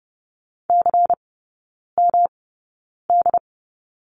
Morse code, heard as a keyed tone, about 30 wpm, spaced as for 9 wpm, 700 Hz. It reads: CMD
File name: Lesson 18e - Letter m - Three Characters 30wpm